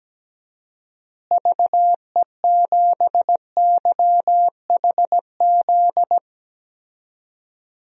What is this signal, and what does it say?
Morse code, heard as a keyed tone, about 17 wpm, 700 Hz. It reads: VE7YHZ
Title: DXCC-100 Call Signs - 6 Characters - Repeated 3x - 17wpm